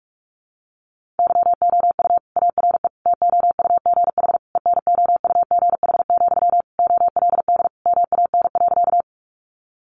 Morse code, heard as a keyed tone, about 44 wpm, 700 Hz. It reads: YOU ARE TOUGH ENOUGH, OLD MAN.
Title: Sentences from Top 1200 Words 45wpm